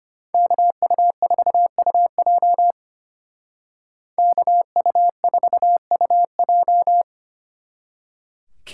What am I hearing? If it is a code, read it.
Morse code, heard as a keyed tone, about 32 wpm, 700 Hz. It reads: KU4UJ KU4UJ